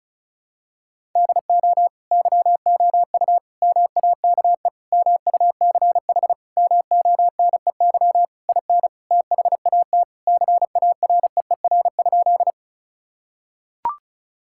Morse code, heard as a keyed tone, about 36 wpm, 700 Hz. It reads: DO YOU MAKE MUCH MONEY IN THAT CAREER?
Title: Sentences from Top 1000 Words - Review Part 6 - 35wpm